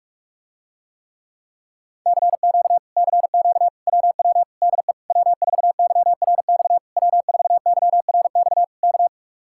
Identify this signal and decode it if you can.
Morse code, heard as a keyed tone, about 45 wpm, 700 Hz. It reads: CQ CQ WW DE W4YRX W4YRX K